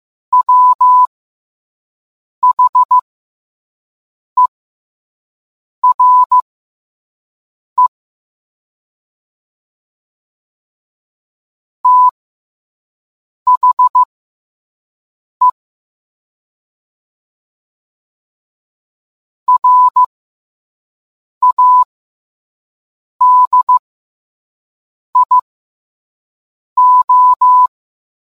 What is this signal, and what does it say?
Morse code, heard as a keyed tone, about 15 wpm, spaced as for 5 wpm, 1000 Hz. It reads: WHERE THE RADIO